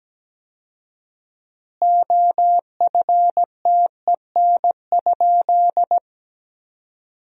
Morse code, heard as a keyed tone, about 17 wpm, 700 Hz. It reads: OFTEN?
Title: 4x Word-Spacing - Sentences from Top 400 Words - 17wpm